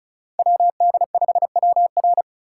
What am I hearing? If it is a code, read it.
Morse code, heard as a keyed tone, about 45 wpm, 700 Hz. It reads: WD5WR